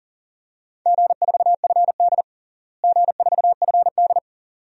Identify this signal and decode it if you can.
Morse code, heard as a keyed tone, about 40 wpm, 700 Hz. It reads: G4FD G4FD
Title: DXCC-100 Call Signs - 4 Characters - Repeated 2x - 40wpm